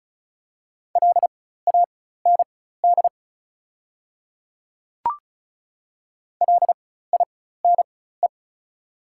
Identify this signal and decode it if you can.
Morse code, heard as a keyed tone, about 36 wpm, spaced as for 16 wpm, 700 Hz. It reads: LAND LINE
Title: 4x Character Spacing - Top 200 Words - 35wpm